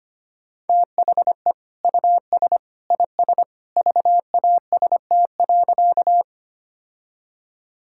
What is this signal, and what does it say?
Morse code, heard as a keyed tone, about 25 wpm, 700 Hz. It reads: THE US IS VAST.
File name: practice / 25wpm / Sentences with 2-Word Phrases - 25wpm